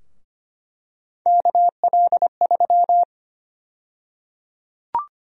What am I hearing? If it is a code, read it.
Morse code, heard as a keyed tone, about 25 wpm, 700 Hz. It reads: KL3
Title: USA Call Sign Prefixes - 25wpm